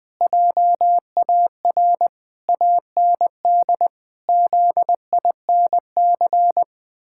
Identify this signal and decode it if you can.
Morse code, heard as a keyed tone, about 20 wpm, 700 Hz. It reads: JAR AND ZINC
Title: Pangram Challenge v2 - 20wpm